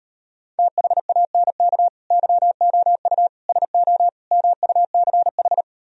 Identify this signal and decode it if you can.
Morse code, heard as a keyed tone, about 33 wpm, 700 Hz. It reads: THANK YOU SO MUCH